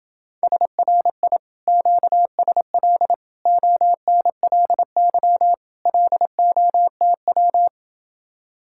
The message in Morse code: SRI QSL ONLY LOTW

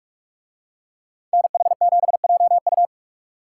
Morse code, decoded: NH7JU

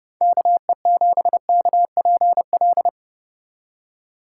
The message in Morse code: KE7KPL

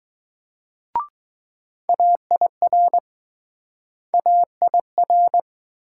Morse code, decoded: AIR AIR

